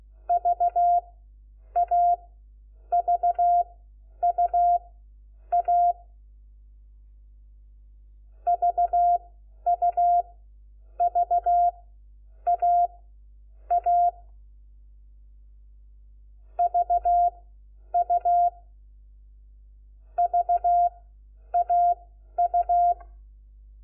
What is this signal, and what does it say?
Morse code, heard as a keyed tone, about 16 wpm, spaced as for 9 wpm, 700 Hz. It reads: VAVUA VUVAA VU VAU
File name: Lecciones / CLASE Nº4